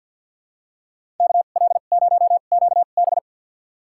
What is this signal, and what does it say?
Morse code, heard as a keyed tone, about 50 wpm, spaced as for 36 wpm, 700 Hz. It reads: KL0QB
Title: USA Call Signs - 2x Character Spacing - 50wpm